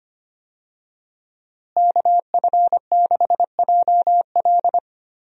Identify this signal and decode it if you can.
Morse code, heard as a keyed tone, about 25 wpm, 700 Hz. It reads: KF6JL
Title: USA Call Signs - 5 Characters - Repeated 3x - 25wpm